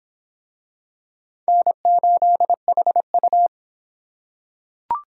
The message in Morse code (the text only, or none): N8HU